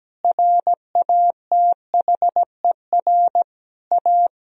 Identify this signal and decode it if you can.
Morse code, heard as a keyed tone, about 17 wpm, 700 Hz. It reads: RATHER A
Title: Sentences from Top 1000 Words - Review Part 4 - 17wpm